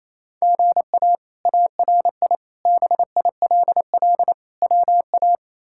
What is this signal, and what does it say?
Morse code, heard as a keyed tone, about 28 wpm, 700 Hz. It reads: GA ARI BILL WA
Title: K1USN SST QSO Practice - 28wpm